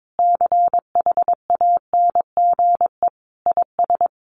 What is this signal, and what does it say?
Morse code, heard as a keyed tone, about 22 wpm, 700 Hz. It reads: CHANGE IS
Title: Sentences from Top 1500 Words 22wpm